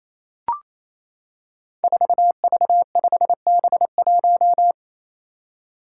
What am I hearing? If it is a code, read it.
Morse code, heard as a keyed tone, about 28 wpm, 700 Hz. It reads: 4V5B1
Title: Five Challenging Letter-Number Combinations 28wpm